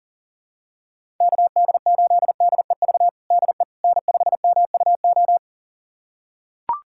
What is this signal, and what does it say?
Morse code, heard as a keyed tone, about 40 wpm, 700 Hz. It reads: KD8DEV DE N5MUO
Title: USA Callsign DE Callsign 40wpm